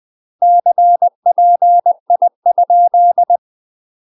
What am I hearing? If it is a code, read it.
Morse code, heard as a keyed tone, about 20 wpm, 700 Hz. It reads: CPI?
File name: UK MASTER QSOs TRAINING / 20WPM / UK MASTER QSOs 0029